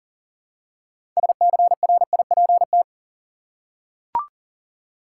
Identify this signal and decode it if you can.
Morse code, heard as a keyed tone, about 39 wpm, 700 Hz. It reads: SCRIPT
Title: Six Letter Words - 40wpm